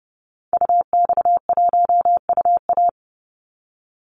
Morse code, heard as a keyed tone, about 30 wpm, 700 Hz. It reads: UX1UA
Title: Call Signs Repeated 3x - 30wpm